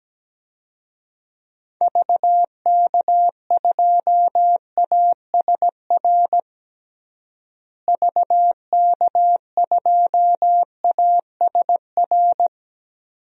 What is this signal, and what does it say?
Morse code, heard as a keyed tone, about 17 wpm, 700 Hz. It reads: VK2ASR VK2ASR
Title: DXCC-100 Call Signs - 6 Characters - Repeated 2x - 17wpm